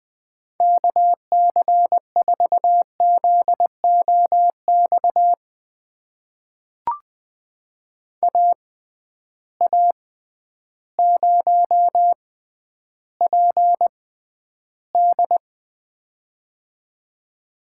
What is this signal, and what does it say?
Morse code, heard as a keyed tone, about 20 wpm, spaced as for 11 wpm, 700 Hz. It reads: KC4ZOX AA0PD
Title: USA Call Signs - 6x Character Spacing - 20wpm